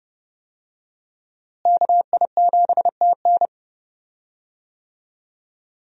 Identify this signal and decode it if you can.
Morse code, heard as a keyed tone, about 30 wpm, 700 Hz. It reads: KI7TN
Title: USA Call Signs - 5 Characters - Repeated 2x - 30wpm